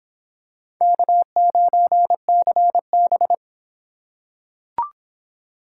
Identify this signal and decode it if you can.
Morse code, heard as a keyed tone, about 26 wpm, 700 Hz. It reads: K9CB